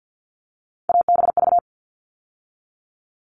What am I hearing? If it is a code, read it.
Morse code, heard as a keyed tone, about 50 wpm, 700 Hz. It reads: ABV
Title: All Cut Numbers - 3-Digit 50wpm